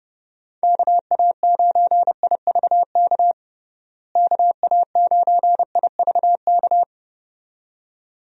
Morse code, heard as a keyed tone, about 30 wpm, 700 Hz. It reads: KA9IVK KA9IVK
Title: USA Call Signs - 6 Characters - Repeated 2x - 30wpm